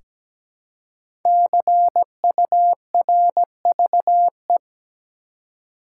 Morse code, heard as a keyed tone, about 17 wpm, 700 Hz. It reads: CURVE
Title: Five Letter Words - 17wpm